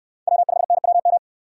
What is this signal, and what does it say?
Morse code, heard as a keyed tone, about 65 wpm, 700 Hz. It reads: W5NWM